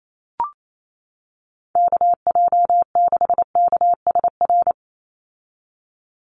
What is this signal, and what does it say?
Morse code, heard as a keyed tone, about 28 wpm, 700 Hz. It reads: KJ6KSR
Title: USA Call Signs 28wpm